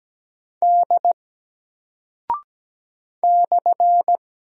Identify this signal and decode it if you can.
Morse code, heard as a keyed tone, about 17 wpm, 700 Hz. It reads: D /
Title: Single Character (Letter:Number:Punctuation:Prosign) 17wpm